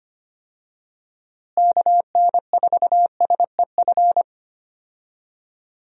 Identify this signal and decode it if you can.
Morse code, heard as a keyed tone, about 25 wpm, 700 Hz. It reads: KN4SEF
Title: USA Call Signs - 6 Characters - Repeated 2x - 25wpm